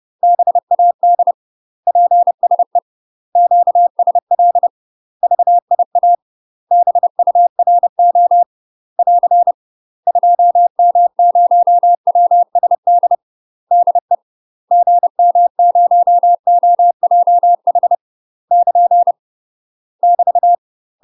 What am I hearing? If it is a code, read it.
Morse code, heard as a keyed tone, about 30 wpm, 700 Hz. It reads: BAD PSE QSL VIA BURO <AR> 2M0WSD DE GM0OJH <KN> <BT>
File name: UK MASTER QSOs TRAINING/30WPM